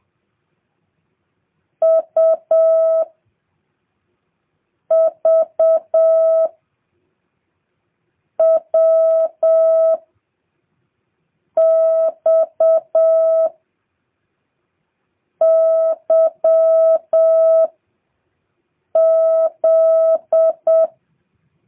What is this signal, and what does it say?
Morse code, heard as a keyed tone, about 7 wpm, 660 Hz. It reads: U V W X Y Z